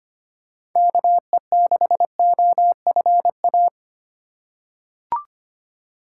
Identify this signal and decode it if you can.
Morse code, heard as a keyed tone, about 25 wpm, 700 Hz. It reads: KE6OFA
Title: USA Call Signs - 6 Characters - 4x Character Spacing - 25wpm